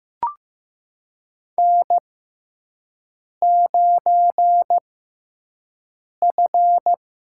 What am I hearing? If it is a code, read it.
Morse code, heard as a keyed tone, about 15 wpm, spaced as for 5 wpm, 700 Hz. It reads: N9F
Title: USA Call Signs - 5 Characters - 6x Character Spacing - 15wpm